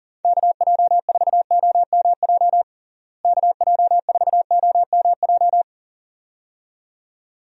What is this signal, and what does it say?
Morse code, heard as a keyed tone, about 40 wpm, 700 Hz. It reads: KJ4OMJ KJ4OMJ
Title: USA Call Signs - 6 Characters - Repeated 2x - 40wpm